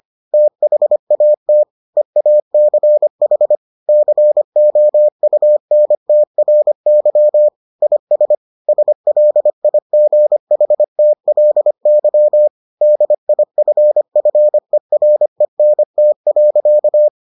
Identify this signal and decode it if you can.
Morse code, heard as a keyed tone, about 25 wpm, 600 Hz. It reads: THAT EACH COUNTRY IS SLIGHTLY DIFFERENT.